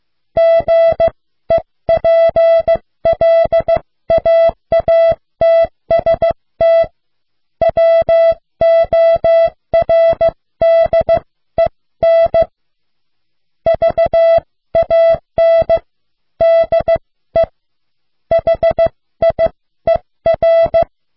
Morse code, heard as a keyed tone, about 15 wpm, spaced as for 13 wpm, 660 Hz. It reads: GEPLAATST WORDEN VAN DE HIER